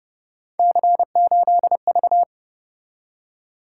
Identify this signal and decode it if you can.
Morse code, heard as a keyed tone, about 30 wpm, 700 Hz. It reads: C8V
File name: Speed-Racing - Three Letter-Number Combinations - 25wpm